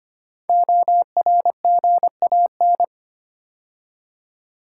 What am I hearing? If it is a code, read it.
Morse code, heard as a keyed tone, about 25 wpm, 700 Hz. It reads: ORGAN
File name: 5 Letter Words - First 3 Letters Given - 25wpm